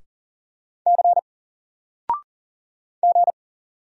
Morse code, heard as a keyed tone, about 40 wpm, 700 Hz. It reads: C G
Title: Single Letter-Number Alt 40wpm